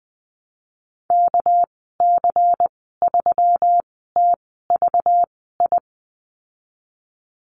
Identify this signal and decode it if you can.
Morse code, heard as a keyed tone, about 20 wpm, spaced as for 15 wpm, 700 Hz. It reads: KC3TVI
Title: 219 Lesson 35g - Number 0 - Call Signs 20wpm